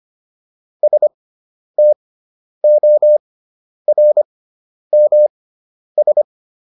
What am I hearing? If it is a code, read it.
Morse code, heard as a keyed tone, about 25 wpm, spaced as for 10 wpm, 600 Hz. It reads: STORMS